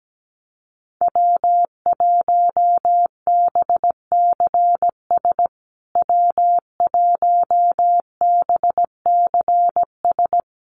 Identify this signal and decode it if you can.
Morse code, heard as a keyed tone, about 17 wpm, 700 Hz. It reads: W1BCS W1BCS
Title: USA Call Signs Repeated 3x 17wpm